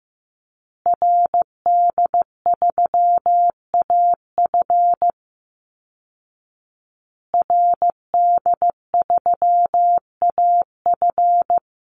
Morse code, heard as a keyed tone, about 15 wpm, 700 Hz. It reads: RD3AF RD3AF
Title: Call Signs Repeated 2x - 15wpm